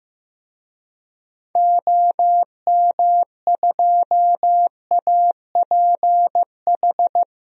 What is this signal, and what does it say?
Morse code, heard as a keyed tone, about 15 wpm, 700 Hz. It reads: OM2APH
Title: DXCC-100 Call Signs - 6 Characters - 15wpm